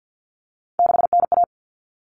Morse code, heard as a keyed tone, about 48 wpm, 700 Hz. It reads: 6NA
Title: Common Cut Numbers - 3-Digit 50wpm